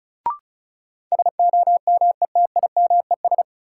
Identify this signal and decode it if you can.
Morse code, heard as a keyed tone, about 36 wpm, 700 Hz. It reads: SOMETIMES
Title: Sentences from Top 600 Words - New Word Drill - 35wpm